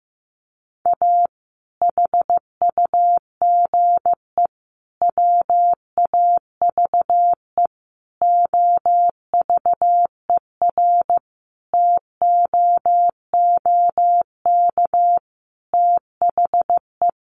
Morse code, heard as a keyed tone, about 15 wpm, 700 Hz. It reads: A HUGE WAVE OVERTOOK THE
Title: Sentences from Top 1400 Words 15wpm